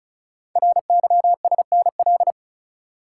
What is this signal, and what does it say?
Morse code, heard as a keyed tone, about 36 wpm, 700 Hz. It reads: RYSNL
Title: Code Groups - 5 Letters - 35wpm